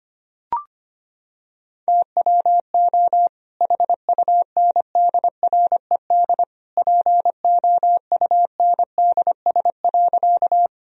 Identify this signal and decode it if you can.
Morse code, heard as a keyed tone, about 25 wpm, 700 Hz. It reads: TWO HUNDRED POUNDS.
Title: Sentences from Top 1000 Words - Review Part 2 - 25wpm